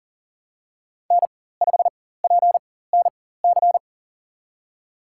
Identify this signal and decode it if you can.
Morse code, heard as a keyed tone, about 40 wpm, spaced as for 19 wpm, 700 Hz. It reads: N5PNC
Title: USA Call Signs - 5 Characters - 4x Character Spacing - 40wpm